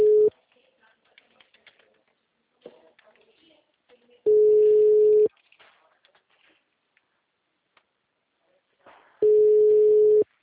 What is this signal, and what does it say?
Ringback tone (Europe, 425 Hz).